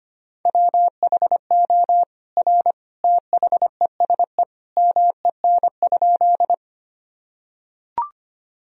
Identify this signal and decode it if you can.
Morse code, heard as a keyed tone, about 25 wpm, 700 Hz. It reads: WHO R THESE MEN?
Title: Sentences from Top 200 Words and Abbreviations 25wpm